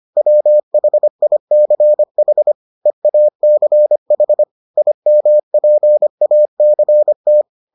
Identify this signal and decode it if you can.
Morse code, heard as a keyed tone, about 25 wpm, 600 Hz. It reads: WHICH EACH IMPACT